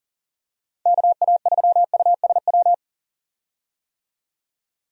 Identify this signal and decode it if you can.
Morse code, heard as a keyed tone, about 40 wpm, 700 Hz. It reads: KA3USW
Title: USA Call Signs - 6 Characters - 40wpm